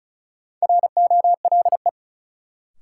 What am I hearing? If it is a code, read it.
Morse code, heard as a keyed tone, about 34 wpm, 700 Hz. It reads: ROLE
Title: Sentences from Top 1500 Words - New Word Drill Alt - 35wpm